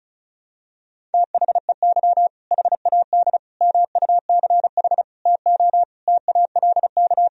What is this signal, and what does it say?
Morse code, heard as a keyed tone, about 36 wpm, 700 Hz. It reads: THEY HAD MUCH TO TALK
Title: Sentences from Top 400 Words and Abbreviations 35wpm